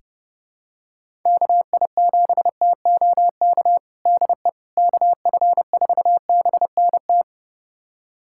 Morse code, heard as a keyed tone, about 30 wpm, 700 Hz. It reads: KI7TOK DE KF4BNT
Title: USA Callsign DE Callsign - 6 Characters - 30wpm